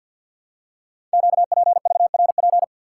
Morse code, heard as a keyed tone, about 50 wpm, 700 Hz. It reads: QPVRP
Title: Code Groups - 5 Letters - 50wpm